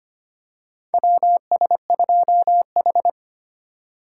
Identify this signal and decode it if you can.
Morse code, heard as a keyed tone, about 25 wpm, 700 Hz. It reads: WS2H
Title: USA Call Signs - 4 Characters - 2x Character Spacing - 25wpm